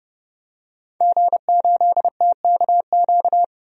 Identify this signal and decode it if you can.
Morse code, heard as a keyed tone, about 30 wpm, 700 Hz. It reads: G8TKQ
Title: DXCC-100 Call Signs - 5 Characters - Repeated 2x - 30wpm